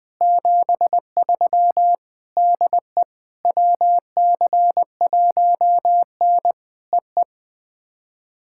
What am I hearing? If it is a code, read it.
Morse code, heard as a keyed tone, about 19 wpm, 700 Hz. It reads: 73 DE WC1N EE
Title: POTA QSO Practice - 20wpm